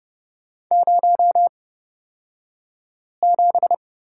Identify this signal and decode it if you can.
Morse code, heard as a keyed tone, about 30 wpm, spaced as for 5 wpm, 700 Hz. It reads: 07
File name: Lesson 38d - Letter q - Two Characters 30wpm